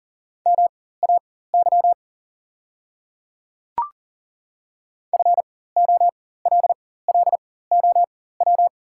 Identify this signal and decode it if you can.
Morse code, heard as a keyed tone, about 40 wpm, spaced as for 19 wpm, 690 Hz. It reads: MAY FOLLOW
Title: 4x Character Spacing - Top 300 Words - 40wpm